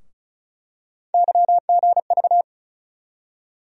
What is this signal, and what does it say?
Morse code, heard as a keyed tone, about 34 wpm, 700 Hz. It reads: YGV
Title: USA Call Sign Postfixes - 3 Letters 35wpm